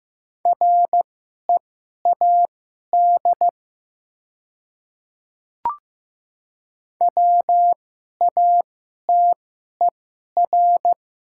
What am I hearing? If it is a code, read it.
Morse code, heard as a keyed tone, about 15 wpm, spaced as for 11 wpm, 700 Hz. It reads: READ WATER
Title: 2x Character Spacing - Top 300 Words - 15wpm